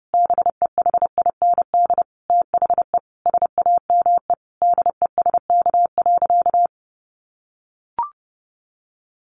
Morse code, heard as a keyed tone, about 30 wpm, 700 Hz. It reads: BEHIND THE SAME DESK.
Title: A Sound Of Thunder 30wpm